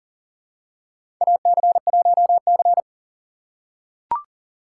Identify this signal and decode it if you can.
Morse code, heard as a keyed tone, about 38 wpm, 700 Hz. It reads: AC1C
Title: USA Call Signs - 4 Characters - 6x Character Spacing - 40wpm